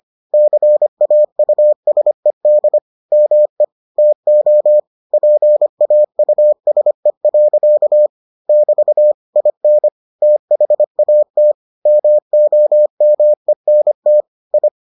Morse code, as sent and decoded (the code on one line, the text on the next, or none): -.-. .- ..- ... -.. / -- . / - --- / .--. .- ..- ... .-.-.- / -...- / .. -. / - .... .- - / -- --- -- . -. - / ..
CAUSED ME TO PAUSE. <BT> IN THAT MOMENT I